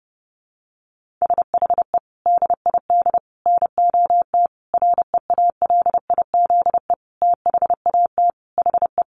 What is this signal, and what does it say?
Morse code, heard as a keyed tone, about 30 wpm, 700 Hz. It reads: SHE DID NOT REALIZE THAT HE
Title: Sentences from Top 1000 Words 30wpm